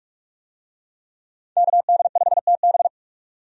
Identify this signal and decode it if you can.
Morse code, heard as a keyed tone, about 46 wpm, 700 Hz. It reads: KD5TB
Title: USA Call Signs - 5 Characters - 45wpm